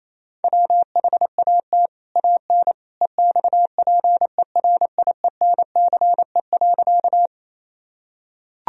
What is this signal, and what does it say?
Morse code, heard as a keyed tone, about 27 wpm, 700 Hz. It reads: WHAT AN EXPERIENCE.
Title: Sentences from Top 1000 Words - Review Part 3 - 28wpm